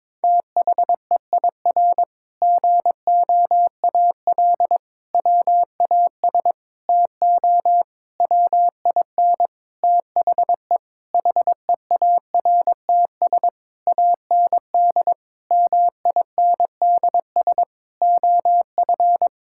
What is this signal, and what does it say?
Morse code, heard as a keyed tone, about 22 wpm, 700 Hz. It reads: THEIR GOAL WAS TO WIN THE HEARTS AND MINDS OF
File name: practice / 22wpm / Sentences with 8-Word Phrases - 22wpm